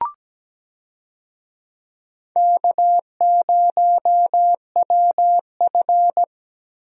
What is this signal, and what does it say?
Morse code, heard as a keyed tone, about 17 wpm, 700 Hz. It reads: K0WF
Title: USA Call Signs - 4 Characters - Repeated 3x - 17wpm